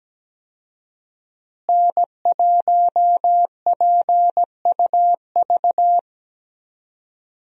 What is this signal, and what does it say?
Morse code, heard as a keyed tone, about 17 wpm, 700 Hz. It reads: N1PUV
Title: USA Call Signs - 5 Characters - Repeated 2x - 17wpm